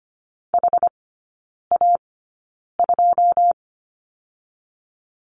Morse code, heard as a keyed tone, about 25 wpm, spaced as for 9 wpm, 700 Hz. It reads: HA2
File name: Lesson 14e - Number 2 - Three Characters 25wpm